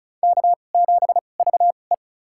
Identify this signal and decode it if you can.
Morse code, heard as a keyed tone, about 36 wpm, spaced as for 25 wpm, 700 Hz. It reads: K7VE
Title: USA Call Signs - 2x Character Spacing - 35wpm